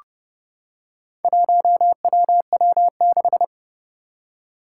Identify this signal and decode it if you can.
Morse code, heard as a keyed tone, about 30 wpm, 700 Hz. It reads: 1WW6
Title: Four Challenging Letter-Number Combinations 30wpm